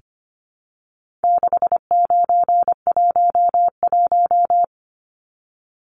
Morse code, heard as a keyed tone, about 25 wpm, 700 Hz. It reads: 6911
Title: Numbers - 4-Digit 25wpm